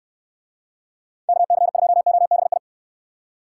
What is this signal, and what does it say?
Morse code, heard as a keyed tone, about 69 wpm, 700 Hz. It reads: KC2OBI